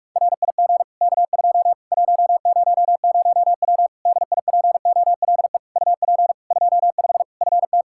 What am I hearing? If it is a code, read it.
Morse code, heard as a keyed tone, about 46 wpm, 700 Hz. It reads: RIG K2 100W DIPOLE UP 25 FT
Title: Practice QSO - Exchange 2 - Info - 45wpm